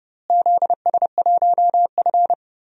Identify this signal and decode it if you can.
Morse code, heard as a keyed tone, about 32 wpm, 700 Hz. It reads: ZS1F
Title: DXCC-100 Call Signs - 4 Characters - Repeated 3x - 30wpm